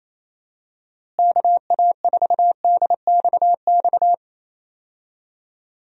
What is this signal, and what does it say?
Morse code, heard as a keyed tone, about 28 wpm, 700 Hz. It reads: KA4DXX
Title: USA Call Signs - 6 Characters - Repeated 3x - 28wpm